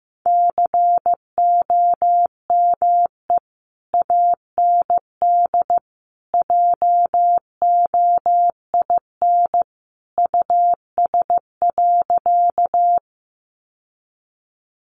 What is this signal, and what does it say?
Morse code, heard as a keyed tone, about 15 wpm, 700 Hz. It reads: COME AND JOIN US.
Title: Sentences from Top 1400 Words 15wpm